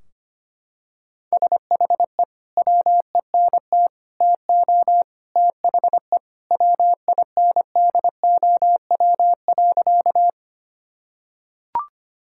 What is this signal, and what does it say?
Morse code, heard as a keyed tone, about 25 wpm, 700 Hz. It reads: SHE WENT TO THE WINDOW.
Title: Sentences from Top 1600 Words 25wpm